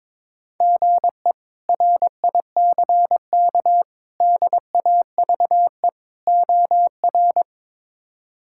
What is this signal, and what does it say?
Morse code, heard as a keyed tone, about 22 wpm, 700 Hz. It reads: GE RICK DAVE OR
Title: K1USN SST QSO Practice - 22wpm